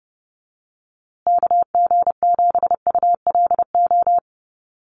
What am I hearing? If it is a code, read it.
Morse code, heard as a keyed tone, about 30 wpm, 700 Hz. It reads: KG7ULO
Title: USA Call Signs Repeated 3x 30wpm